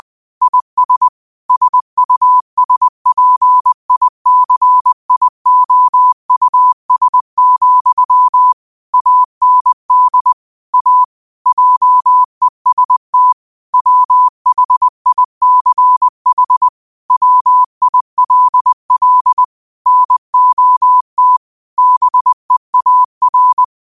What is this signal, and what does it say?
Morse code, heard as a keyed tone, about 20 wpm, 1000 Hz. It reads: IS SUSPICIOUS, AND A JEST WHICH WILL NOT BEAR
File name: qotdfiles / 20wpm